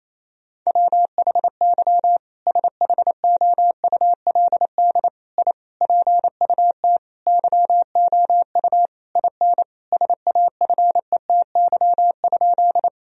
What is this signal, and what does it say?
Morse code, heard as a keyed tone, about 28 wpm, 700 Hz. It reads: WHY SHOULD I PUT YOU IN SAFETY?
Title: Sentences from Top 1000 Words - Review Part 4 - 28wpm